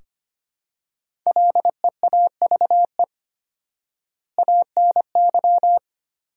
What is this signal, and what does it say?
Morse code, heard as a keyed tone, about 25 wpm, 700 Hz. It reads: LEAVE ANY
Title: 4x Word-Spacing - Sentences from Top 400 Words - 25wpm